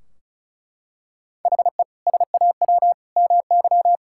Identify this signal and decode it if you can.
Morse code, heard as a keyed tone, about 34 wpm, 700 Hz. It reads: HE SAW MY
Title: Sentences from Top 1600 Words 35wpm